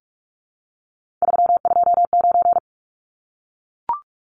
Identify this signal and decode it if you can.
Morse code, heard as a keyed tone, about 44 wpm, 700 Hz. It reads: 329
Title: Numbers - 3-Digit 45wpm